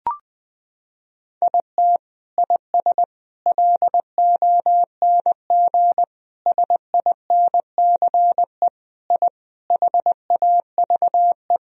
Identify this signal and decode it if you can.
Morse code, heard as a keyed tone, about 20 wpm, 700 Hz. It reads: IT IS LONG SINCE I HAVE
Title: Sentences from Top 1000 Words - Review Part 1 - 20wpm